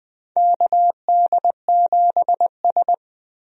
Morse code, heard as a keyed tone, about 20 wpm, 700 Hz. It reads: KD7S
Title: USA Call Signs - 4 Characters - Repeated 3x - 20wpm